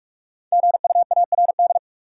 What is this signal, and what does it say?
Morse code, heard as a keyed tone, about 46 wpm, 700 Hz. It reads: GUARD
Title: Sentences from Top 700 Words - New Word Drill - 45wpm